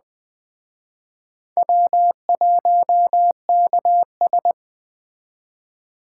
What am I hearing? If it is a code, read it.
Morse code, heard as a keyed tone, about 21 wpm, 700 Hz. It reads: W1KS
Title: Speed Racing - USA 1x2 Call Signs 17wpm